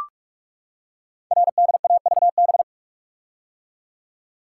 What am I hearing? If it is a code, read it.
Morse code, heard as a keyed tone, about 44 wpm, 700 Hz. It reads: RDAVB
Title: Code Groups - 5 Letters - 45wpm